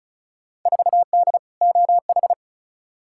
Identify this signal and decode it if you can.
Morse code, heard as a keyed tone, about 36 wpm, 690 Hz. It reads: 4D OH
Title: Field Day Exchanges Encore - 35wpm